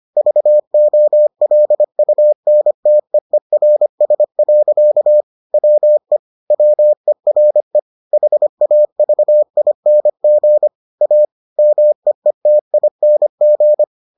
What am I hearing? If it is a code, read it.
Morse code, heard as a keyed tone, about 25 wpm, 600 Hz. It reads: VOLUNTEERS. WE WERE HAVING A MEETING